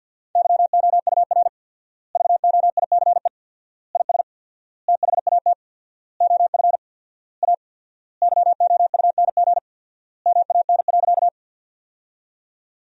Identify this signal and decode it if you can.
Morse code, heard as a keyed tone, about 49 wpm, 700 Hz. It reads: YOUR VOICE IS THAT OF A YOUNG MAN.
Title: 4x Word-Spacing - Sentences from Top 500 Words - 50wpm